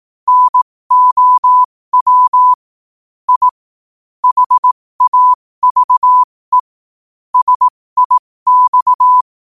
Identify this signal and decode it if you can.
Morse code, heard as a keyed tone, about 18 wpm, spaced as for 15 wpm, 1000 Hz. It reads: NOW I HAVE SIX
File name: qotdfiles / 15wpm